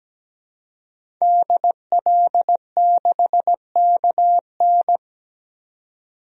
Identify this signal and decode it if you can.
Morse code, heard as a keyed tone, about 17 wpm, 700 Hz. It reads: DL6KN
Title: DXCC-100 Call Signs - 5 Characters - 17wpm